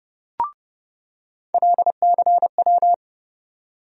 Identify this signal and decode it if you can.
Morse code, heard as a keyed tone, about 32 wpm, 700 Hz. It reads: LCW
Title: USA Call Sign Postfixes - 3 Letters 30wpm